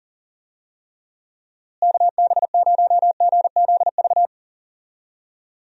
Morse code, heard as a keyed tone, about 40 wpm, 700 Hz. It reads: KB0GZV
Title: USA Call Signs - 6 Characters - Repeated 3x - 40wpm